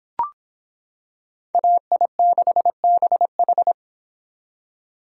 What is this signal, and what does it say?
Morse code, heard as a keyed tone, about 25 wpm, 700 Hz. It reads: AI6BH